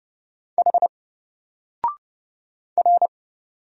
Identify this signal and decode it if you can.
Morse code, heard as a keyed tone, about 30 wpm, 700 Hz. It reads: H R
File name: Single-Letter-Number-30wpm